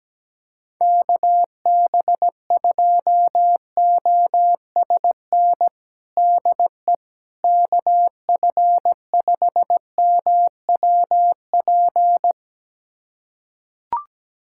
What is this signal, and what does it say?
Morse code, heard as a keyed tone, about 17 wpm, 700 Hz. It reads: KB2OSN DE KF5MWP
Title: USA Callsign DE Callsign 17wpm